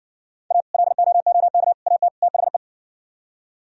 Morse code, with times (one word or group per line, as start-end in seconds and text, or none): 0.5-0.6 A
0.7-1.7 LOOK
1.9-2.1 AT
2.2-2.6 THE